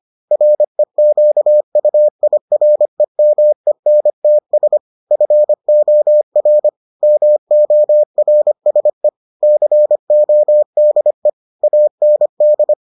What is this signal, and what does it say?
Morse code, heard as a keyed tone, about 25 wpm, 600 Hz. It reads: REQUIREMENTS FOR MORSE CODE AND